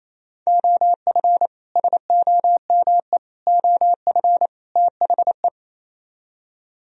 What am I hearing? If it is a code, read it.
Morse code, heard as a keyed tone, about 36 wpm, 700 Hz. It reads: OF SOME OF THE